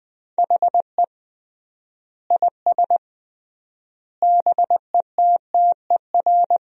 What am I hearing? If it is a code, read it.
Morse code, heard as a keyed tone, about 20 wpm, 700 Hz. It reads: HE IS BETTER
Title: 3x Word-Spacing - Sentences from Top 400 Words - 20wpm